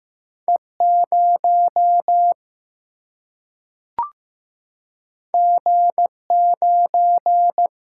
Morse code, heard as a keyed tone, about 15 wpm, 700 Hz. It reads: E 0 G 9
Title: Single-Letter Single-Number 15wpm